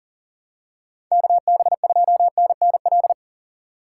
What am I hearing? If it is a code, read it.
Morse code, heard as a keyed tone, about 40 wpm, 700 Hz. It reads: KB2NNL
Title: USA Call Signs - 2x Character Spacing - 40wpm